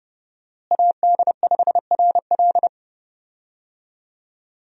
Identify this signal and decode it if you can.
Morse code, heard as a keyed tone, about 30 wpm, 700 Hz. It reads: AD5RL
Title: USA Call Signs - 5 Characters - Repeated 3x - 30wpm